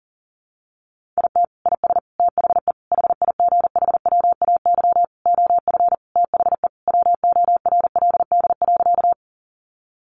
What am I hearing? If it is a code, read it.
Morse code, heard as a keyed tone, about 40 wpm, 700 Hz. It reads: IT IS THE HIGHWAY OF THE WORLD.
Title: Sentences from Top 1400 Words 40wpm